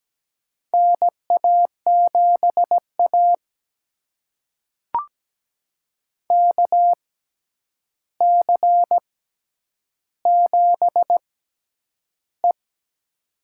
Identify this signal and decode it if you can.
Morse code, heard as a keyed tone, about 17 wpm, spaced as for 9 wpm, 700 Hz. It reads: NA7A KC7E
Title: USA Call Signs - 4 Characters - 6x Character Spacing - 17wpm